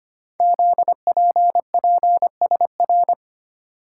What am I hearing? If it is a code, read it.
Morse code, heard as a keyed tone, about 25 wpm, 700 Hz. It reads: ZPPSR
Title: Code Groups - 5 Letters - 25wpm